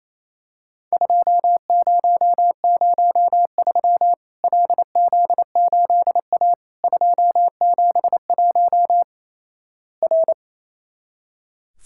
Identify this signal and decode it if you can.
Morse code, heard as a keyed tone, about 28 wpm, 700 Hz. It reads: 2003 LZ8A 271 R
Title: Worked All Europe DX Contest - QTC Exchange - 28wpm